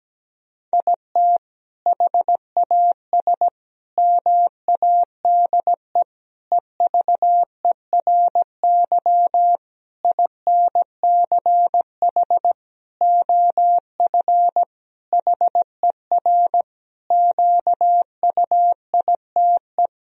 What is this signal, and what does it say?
Morse code, heard as a keyed tone, about 17 wpm, 700 Hz. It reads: IT HAS MADE EVERY INCH OF HER QUITE
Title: Sentences from Top 1000 Words - Review Part 3 - 17wpm